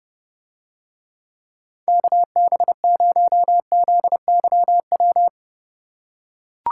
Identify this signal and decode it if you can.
Morse code, heard as a keyed tone, about 30 wpm, 700 Hz. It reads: KB0ZYW